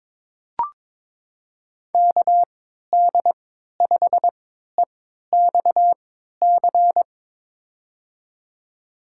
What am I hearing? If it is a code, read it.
Morse code, heard as a keyed tone, about 22 wpm, spaced as for 13 wpm, 700 Hz. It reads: KD5EXC
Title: USA Call Signs - 6 Characters - 3x Character Spacing - 22wpm